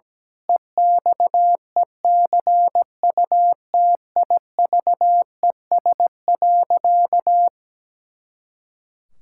Morse code, heard as a keyed tone, about 17 wpm, 700 Hz. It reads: EXECUTIVES.
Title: Pangram Challenge v2 - 17wpm